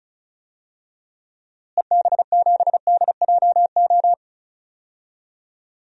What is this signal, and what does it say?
Morse code, heard as a keyed tone, about 36 wpm, 700 Hz. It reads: EB7DJO
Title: DXCC-100 Call Signs - 6 Characters - Repeated 2x - 35wpm